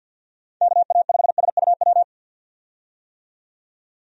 Morse code, heard as a keyed tone, about 52 wpm, 700 Hz. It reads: KA5SUW
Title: USA Call Signs - 6 Characters - 50wpm